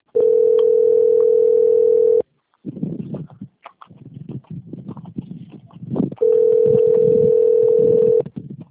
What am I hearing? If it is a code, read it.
Ringback tone (North America), 440 + 480 Hz.